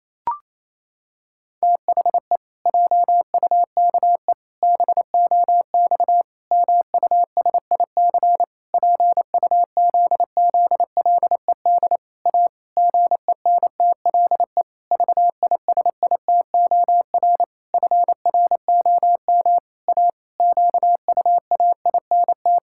Morse code, heard as a keyed tone, about 28 wpm, 700 Hz. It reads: THE JUKE BOX MUSIC PUZZLED A GENTLE VISITOR FROM A QUAINT
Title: Pangram Challenge v2 - 28wpm